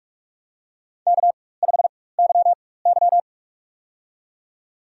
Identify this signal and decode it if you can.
Morse code, heard as a keyed tone, about 46 wpm, spaced as for 21 wpm, 700 Hz. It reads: K5YY
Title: USA Call Signs - 4x Character Spacing - 45wpm